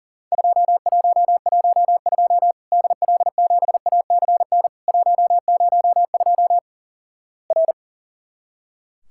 Morse code, heard as a keyed tone, about 37 wpm, 700 Hz. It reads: 2112 DL7ACN 102 R
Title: Worked All Europe DX Contest - QTC Exchange - 40wpm